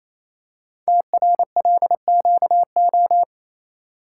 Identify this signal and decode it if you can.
Morse code, heard as a keyed tone, about 27 wpm, 700 Hz. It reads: TRLQO
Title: Code Groups - 5 Letter:Numbers - 28wpm